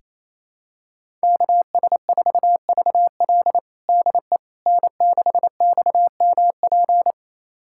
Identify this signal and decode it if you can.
Morse code, heard as a keyed tone, about 27 wpm, 700 Hz. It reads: KS4VL DE N6XMP
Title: USA Callsign DE Callsign - 5 Characters - 28wpm